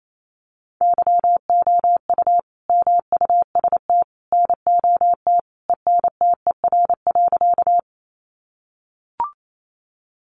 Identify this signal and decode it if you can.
Morse code, heard as a keyed tone, about 28 wpm, 700 Hz. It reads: YOU MUST NOT ENTER.
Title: Sentences from Top 1300 Words 28wpm